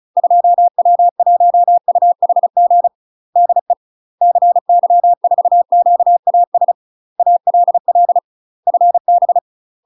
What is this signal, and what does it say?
Morse code, heard as a keyed tone, about 34 wpm, 700 Hz. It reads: 2W1UHG DE CY4QAS ALL FB
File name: UK MASTER QSOs TRAINING / 35WPM / UK MASTER QSOs 0005